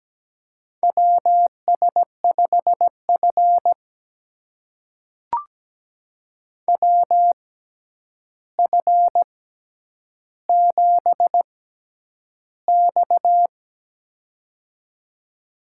Morse code, heard as a keyed tone, about 17 wpm, spaced as for 9 wpm, 700 Hz. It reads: WS5F WF7X
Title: USA Call Signs - 4 Characters - 6x Character Spacing - 17wpm